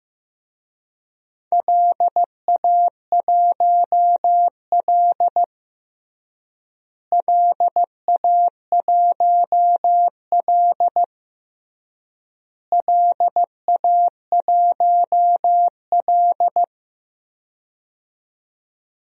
Morse code, heard as a keyed tone, about 15 wpm, 700 Hz. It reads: LA1L LA1L LA1L
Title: DXCC-100 Call Signs - 4 Characters - Repeated 3x - 15wpm